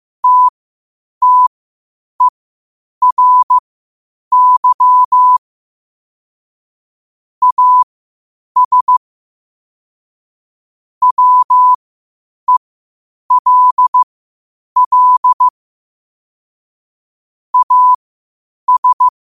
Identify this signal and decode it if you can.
Morse code, heard as a keyed tone, about 15 wpm, spaced as for 8 wpm, 1000 Hz. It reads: TTERY AS WELL AS
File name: qotdfiles / 8wpm